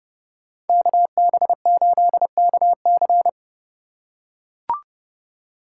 Morse code, heard as a keyed tone, about 29 wpm, 700 Hz. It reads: KB8KC